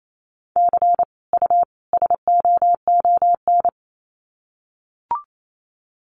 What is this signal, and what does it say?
Morse code, heard as a keyed tone, about 28 wpm, 700 Hz. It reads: C U SOON
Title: Common QSO Phrases 28wpm